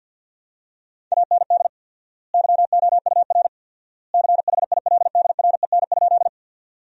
Morse code, heard as a keyed tone, about 50 wpm, 700 Hz. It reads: AND YOUR CHILDREN?
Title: 4x Word-Spacing - Sentences from Top 300 Words - 50wpm